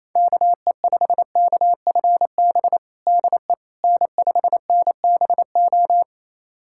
Morse code, heard as a keyed tone, about 28 wpm, 700 Hz. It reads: KE5KFB DE N5NBO
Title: USA Callsign DE Callsign 28wpm